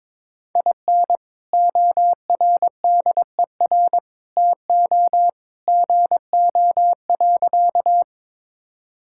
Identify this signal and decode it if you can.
Morse code, heard as a keyed tone, about 22 wpm, 700 Hz. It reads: IN ORDER TO GO.
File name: practice / 22wpm / Sentences with 2-Word Phrases - 22wpm